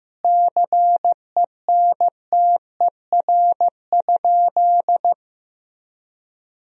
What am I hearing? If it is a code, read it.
Morse code, heard as a keyed tone, about 16 wpm, 700 Hz. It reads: CENTER?
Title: Sentences from Top 1000 Words - Review Part 2 - 15wpm